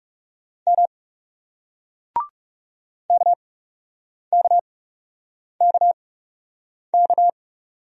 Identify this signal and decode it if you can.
Morse code, heard as a keyed tone, about 39 wpm, spaced as for 8 wpm, 700 Hz. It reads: M KKKK